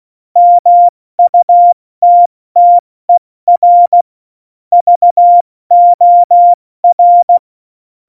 Morse code, heard as a keyed tone, about 16 wpm, spaced as for 14 wpm, 700 Hz. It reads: MUTTER VOR
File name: Morsekurs / Witze / Witz 16. Oktober 2018, 70 F80